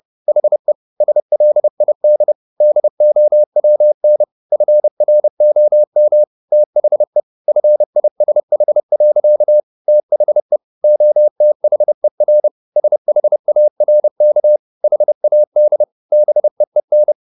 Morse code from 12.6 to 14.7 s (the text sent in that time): SHARK